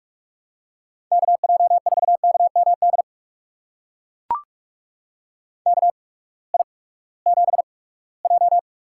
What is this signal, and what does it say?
Morse code, heard as a keyed tone, about 42 wpm, spaced as for 22 wpm, 700 Hz. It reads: KJ4KMD KI7J
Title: USA Call Signs - 6 Characters - 8x Character Spacing - 45wpm